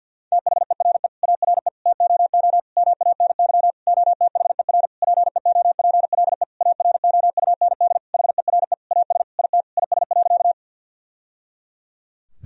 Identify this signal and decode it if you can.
Morse code, heard as a keyed tone, about 50 wpm, 700 Hz. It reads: THERE ARE TOO MANY OTHER PEOPLE AROUND HERE AS IT IS.
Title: Sentences from Top 1000 Words - Review Part 3 - 50wpm